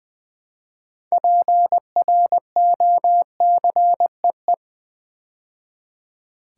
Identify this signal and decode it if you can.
Morse code, heard as a keyed tone, about 20 wpm, 700 Hz. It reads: PROCEE
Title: Build A Word - 2 Letters At A Time - 8-Letter Words - 20wpm